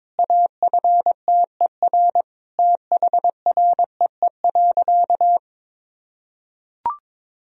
Morse code, heard as a keyed tone, about 23 wpm, 700 Hz. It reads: AFTER THREE.
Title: Sentences from Top 1000 Words - Review Part 1 - 22wpm